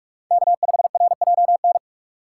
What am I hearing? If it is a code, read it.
Morse code, heard as a keyed tone, about 43 wpm, 700 Hz. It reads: K5RJN
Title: USA Call Signs - 5 Characters - 6x Character Spacing - 45wpm